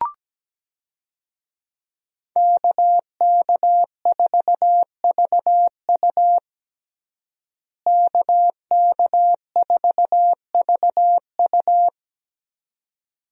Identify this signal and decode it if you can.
Morse code, heard as a keyed tone, about 17 wpm, 700 Hz. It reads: KK4VU KK4VU
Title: USA Call Signs - 5 Characters - Repeated 2x - 17wpm